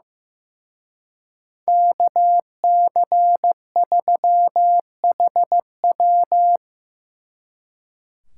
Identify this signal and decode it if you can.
Morse code, heard as a keyed tone, about 18 wpm, 700 Hz. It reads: KC3HW
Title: Speed Racing - USA 2x2 Call Signs 15wpm